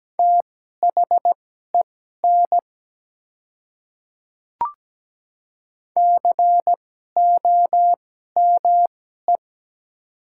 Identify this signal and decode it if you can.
Morse code, heard as a keyed tone, about 17 wpm, spaced as for 12 wpm, 690 Hz. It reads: THEN COME
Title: 2x Character Spacing - Top 100 Words - 17wpm